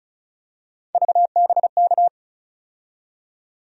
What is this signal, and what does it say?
Morse code, heard as a keyed tone, about 36 wpm, 700 Hz. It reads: VBK
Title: USA Call Sign Postfixes - 3 Letters 35wpm